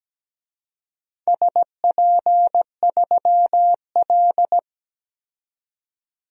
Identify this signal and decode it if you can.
Morse code, heard as a keyed tone, about 17 wpm, 700 Hz. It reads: SP3L
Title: DXCC-100 Call Signs - 4 Characters - 17wpm